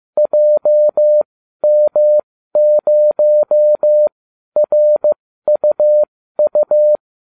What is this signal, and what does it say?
Morse code, heard as a keyed tone, about 15 wpm, spaced as for 12 wpm, 600 Hz. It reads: JM0RUU